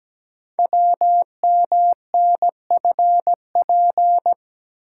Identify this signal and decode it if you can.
Morse code, heard as a keyed tone, about 17 wpm, 700 Hz. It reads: WMNFP
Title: Code Groups - 5 Letters - 17wpm